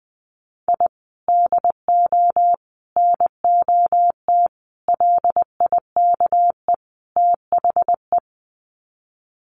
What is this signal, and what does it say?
Morse code, heard as a keyed tone, about 19 wpm, 700 Hz. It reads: I DO NOT LIKE THE
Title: Sets of 5 Words Encore 20wpm